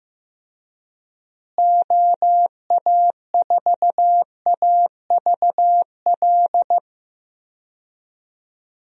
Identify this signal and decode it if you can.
Morse code, heard as a keyed tone, about 15 wpm, 700 Hz. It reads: OA4AVL
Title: USA Call Signs - 6 Characters - Repeated 3x - 15wpm